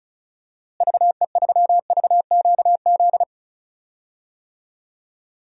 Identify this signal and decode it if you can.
Morse code, heard as a keyed tone, about 35 wpm, 700 Hz. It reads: VE3VQZ
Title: DXCC-100 Call Signs - 6 Characters - Repeated 2x - 35wpm